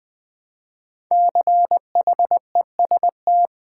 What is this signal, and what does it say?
Morse code, heard as a keyed tone, about 20 wpm, 700 Hz. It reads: CHEST
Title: Build A Word - 1 Letter At A Time - 5-Letter Words - 20wpm